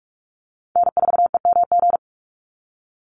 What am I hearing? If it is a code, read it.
Morse code, heard as a keyed tone, about 44 wpm, 690 Hz. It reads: N4EMG
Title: NA SOTA Chaser Callsigns 45wpm